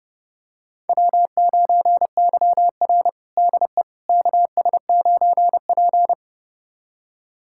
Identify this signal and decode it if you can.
Morse code, heard as a keyed tone, about 31 wpm, 700 Hz. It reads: W9YR DE KS9P
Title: USA Callsign DE Callsign - 4 Characters - 30wpm